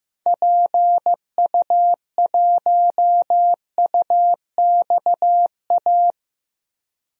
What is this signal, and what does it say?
Morse code, heard as a keyed tone, about 15 wpm, 700 Hz. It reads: PU1UXA
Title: DXCC-100 Call Signs - 6 Characters - 15wpm